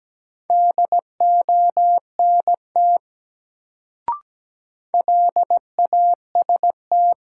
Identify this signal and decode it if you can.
Morse code, heard as a keyed tone, about 17 wpm, 700 Hz. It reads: DONT LAST
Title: Sentences from Top 300 Words - New Word Drill - 17wpm